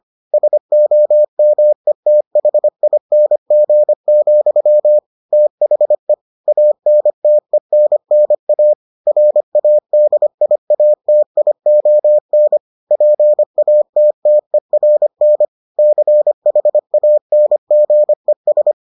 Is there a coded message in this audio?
Yes: Morse code from 0.3 to 18.7 s, heard as a keyed tone, about 25 wpm, 600 Hz. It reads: SOMETHING, THE ANTENNA RADIATION PATTERN CHANGES